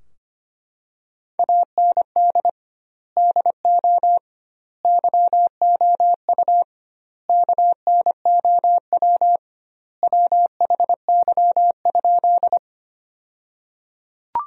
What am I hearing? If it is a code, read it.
Morse code, heard as a keyed tone, about 25 wpm, 700 Hz. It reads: AND DO YOU KNOW WHY?
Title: 2x Word-Spacing - Sentences from Top 200 Words - 25wpm